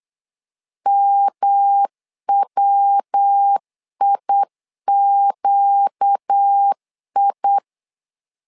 Morse code, heard as a keyed tone, about 8 wpm, 800 Hz. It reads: MWIQI
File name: 4 Q 8wpm